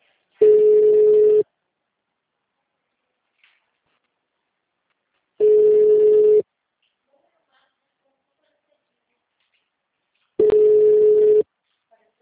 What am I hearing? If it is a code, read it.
Ringback tone (Europe, 425 Hz), with speech.